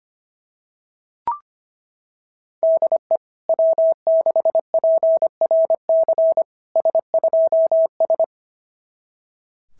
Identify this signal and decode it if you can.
Morse code, heard as a keyed tone, about 25 wpm, 660 Hz. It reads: DE W6PRC S2S
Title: SOTA QSO Practice 25wpm